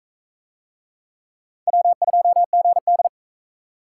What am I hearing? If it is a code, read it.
Morse code, heard as a keyed tone, about 38 wpm, 700 Hz. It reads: W2GD